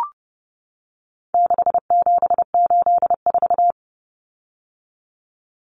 Morse code, heard as a keyed tone, about 30 wpm, 700 Hz. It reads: 6784
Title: Common Cut Numbers - 4-Digit 30wpm